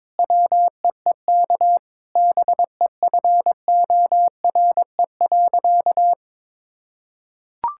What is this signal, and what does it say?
Morse code, heard as a keyed tone, about 21 wpm, 700 Hz. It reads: WEEK BEFORE.
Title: Sentences from Top 1000 Words - Review Part 6 - 22wpm